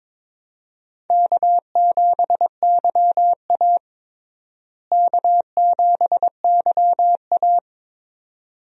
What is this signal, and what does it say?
Morse code, heard as a keyed tone, about 22 wpm, 700 Hz. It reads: K7YA K7YA
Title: USA Call Signs - 4 Characters - Repeated 2x - 22wpm